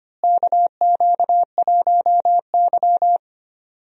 Morse code, heard as a keyed tone, about 25 wpm, 700 Hz. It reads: KQ1Y
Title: USA Call Signs - 4 Characters - Repeated 2x - 25wpm